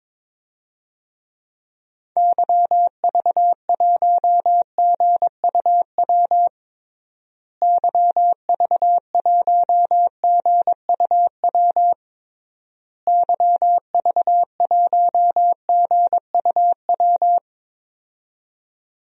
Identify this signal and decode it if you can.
Morse code, heard as a keyed tone, about 22 wpm, 700 Hz. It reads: YV1GUW YV1GUW YV1GUW
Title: DXCC-100 Call Signs - 6 Characters - Repeated 3x - 22wpm